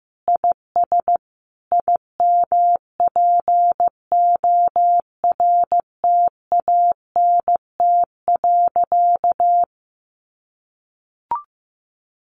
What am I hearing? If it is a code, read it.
Morse code, heard as a keyed tone, about 14 wpm, 700 Hz. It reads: IS IMPORTANT.
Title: Sentences from Top 600 Words 15wpm